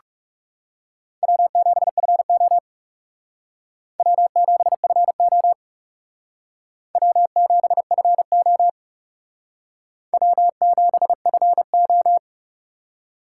Morse code, heard as a keyed tone, about 36 wpm, 700 Hz. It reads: W7FO W7FO W7FO W7FO
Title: Speed Racing - USA 1x2 Call Signs 30wpm